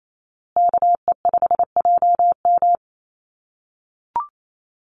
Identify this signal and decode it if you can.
Morse code, heard as a keyed tone, about 28 wpm, 700 Hz. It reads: KE5JM
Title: USA Call Signs Repeated 2x 28wpm